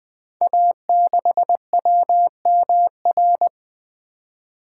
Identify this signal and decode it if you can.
Morse code, heard as a keyed tone, about 20 wpm, 700 Hz. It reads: A6WMR
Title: Code Groups - 5 Letter:Numbers - 20wpm